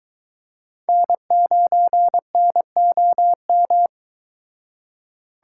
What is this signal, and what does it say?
Morse code, heard as a keyed tone, about 22 wpm, 700 Hz. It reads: N9NOM